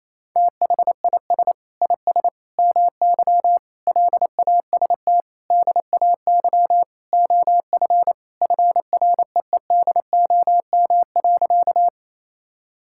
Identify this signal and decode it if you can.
Morse code, heard as a keyed tone, about 28 wpm, 700 Hz. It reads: THIS IS MY LAST DAY OF FREEDOM.
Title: Sentences from Top 1000 Words - Review Part 5 - 28wpm